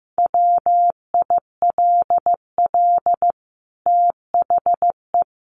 Morse code, heard as a keyed tone, about 15 wpm, 700 Hz. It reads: WILL THE
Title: Sentences from Top 1500 Words 15wpm